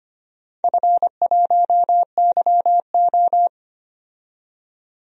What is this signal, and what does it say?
Morse code, heard as a keyed tone, about 25 wpm, 700 Hz. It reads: F1YO
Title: DXCC-100 Call Signs - 4 Characters - 25wpm